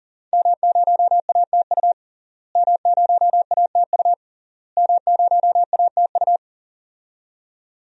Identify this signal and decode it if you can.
Morse code, heard as a keyed tone, about 40 wpm, 700 Hz. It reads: M0ATU M0ATU M0ATU
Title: DXCC-100 Call Signs - 5 Characters - Repeated 3x - 40wpm